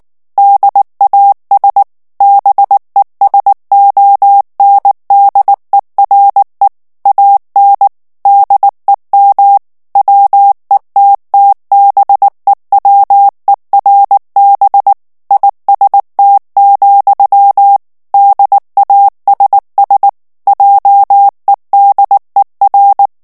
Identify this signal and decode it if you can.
Morse code, heard as a keyed tone, about 19 wpm, 800 Hz. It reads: DAS BESONDERE AN DEM WETTBEWERB IST, DASS JEDER
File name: morse mp3 / ALLEMAND texte clair 2 vitesse 18 mots minute